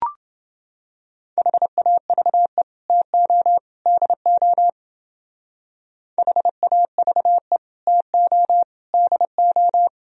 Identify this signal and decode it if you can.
Morse code, heard as a keyed tone, about 29 wpm, 700 Hz. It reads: HAVE TO DO HAVE TO DO